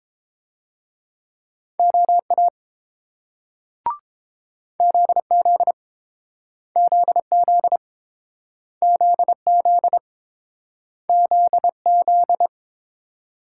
Morse code, heard as a keyed tone, about 30 wpm, 700 Hz. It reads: OA ZZ ZZ ZZ ZZ